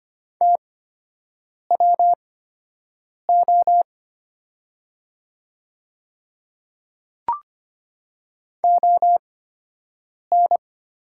Morse code, heard as a keyed tone, about 25 wpm, spaced as for 7 wpm, 700 Hz. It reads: TWO ON
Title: 8x Character Spacing - Top 300 Words - 25wpm